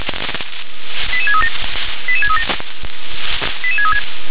Five-tone selective call (ZVEI), heard as quick runs of five tones, tones of about 70 ms. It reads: 88637 88637 88636